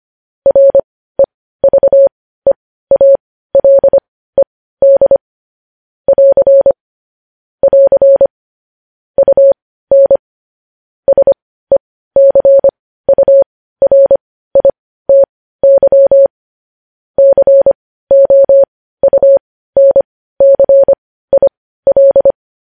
Morse code, heard as a keyed tone, about 25 wpm, spaced as for 15 wpm, 550 Hz. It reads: REVEALED <AR> <AR> UN SECURITY COUNCIL
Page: News Headlines in Morse Code at 15 WPM